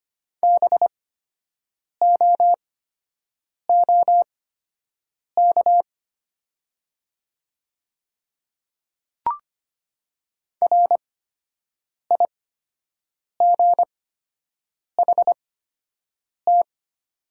Morse code, heard as a keyed tone, about 25 wpm, spaced as for 7 wpm, 700 Hz. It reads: BOOK RIGHT
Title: 8x Character Spacing - Top 300 Words - 25wpm